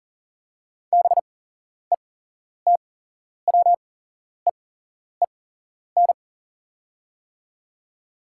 Morse code, heard as a keyed tone, about 40 wpm, spaced as for 11 wpm, 700 Hz. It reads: BETWEEN
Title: 8x Character Spacing - Top 300 Words - 40wpm